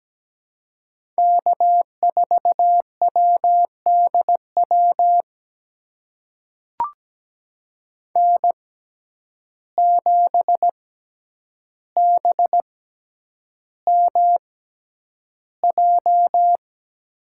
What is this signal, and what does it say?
Morse code, heard as a keyed tone, about 17 wpm, spaced as for 9 wpm, 700 Hz. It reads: K4WDW N7BMJ
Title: USA Call Signs - 5 Characters - 6x Character Spacing - 17wpm